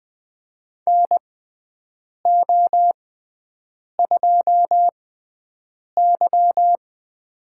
Morse code, heard as a keyed tone, about 20 wpm, spaced as for 7 wpm, 700 Hz. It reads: NO2Y
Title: USA Call Signs - 6x Character Spacing - 20wpm